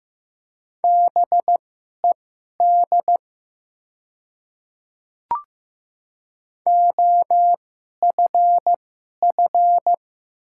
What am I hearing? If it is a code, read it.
Morse code, heard as a keyed tone, about 15 wpm, spaced as for 11 wpm, 700 Hz. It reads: BED OFF
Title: 2x Character Spacing - Top 500 Words - 15wpm